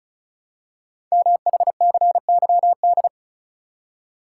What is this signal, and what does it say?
Morse code, heard as a keyed tone, about 36 wpm, 700 Hz. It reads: MHCYD
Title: Code Groups - 5 Letters - 35wpm